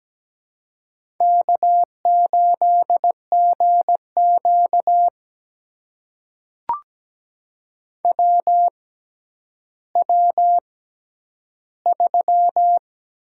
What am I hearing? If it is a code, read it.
Morse code, heard as a keyed tone, about 17 wpm, spaced as for 10 wpm, 700 Hz. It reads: K8GQ WW3
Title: USA Call Signs - 4 Characters - 6x Character Spacing - 17wpm